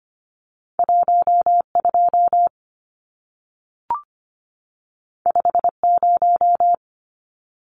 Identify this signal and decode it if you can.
Morse code, heard as a keyed tone, about 25 wpm, 700 Hz. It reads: 12 50
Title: Numbers - 2-Digit 25wpm